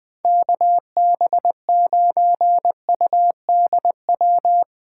Morse code, heard as a keyed tone, about 23 wpm, 700 Hz. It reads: KB9UDW
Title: Speed Racing - USA 2x3 Call Signs 20wpm